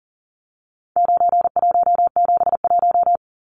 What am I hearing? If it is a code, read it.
Morse code, heard as a keyed tone, about 40 wpm, 700 Hz. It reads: 9171
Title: Numbers - 4-Digit 40wpm